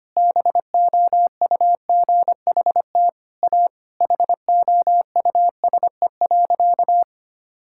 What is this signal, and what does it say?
Morse code, heard as a keyed tone, about 25 wpm, 700 Hz. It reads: BOUGHT A HOUSE.
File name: practice / 25wpm / Sentences with 4-Word Phrases - 25wpm